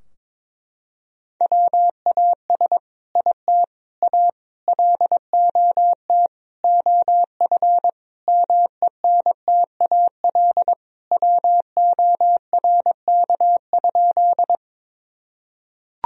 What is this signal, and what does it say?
Morse code, heard as a keyed tone, about 22 wpm, 700 Hz. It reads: WAS IT A LOT OF MENTAL WORK?
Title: Sentences from Top 1000 Words - Review Part 4 - 22wpm